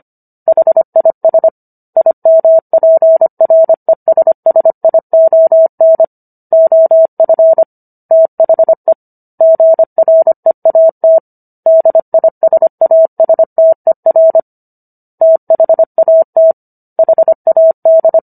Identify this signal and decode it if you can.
Morse code, heard as a keyed tone, about 25 wpm, 650 Hz. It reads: HIS IMPRESSION OF THE GREAT DISASTER THAT HAD